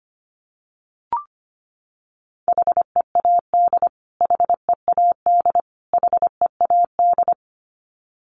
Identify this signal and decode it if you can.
Morse code, heard as a keyed tone, about 25 wpm, 700 Hz. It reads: HEAD HEAD HEAD
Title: Top 500 Words Repeated 25wpm